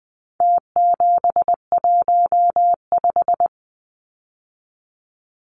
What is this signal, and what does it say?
Morse code, heard as a keyed tone, about 20 wpm, 700 Hz. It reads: T715
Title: Numbers - Contest Style - 4-Digit 20wpm